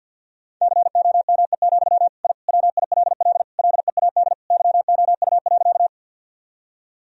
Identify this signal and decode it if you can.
Morse code, heard as a keyed tone, about 51 wpm, 700 Hz. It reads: COME, I WILL LEAD YOU.
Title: Sentences from Top 1000 Words - Review Part 2 - 50wpm